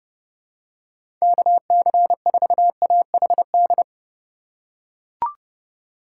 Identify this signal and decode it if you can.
Morse code, heard as a keyed tone, about 32 wpm, 700 Hz. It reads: KC4AHD
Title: USA Call Signs - 6 Characters - 2x Character Spacing - 30wpm